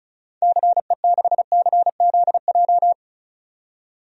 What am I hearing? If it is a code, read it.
Morse code, heard as a keyed tone, about 36 wpm, 700 Hz. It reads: CE6CZJ
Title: DXCC-100 Call Signs - 6 Characters - 35wpm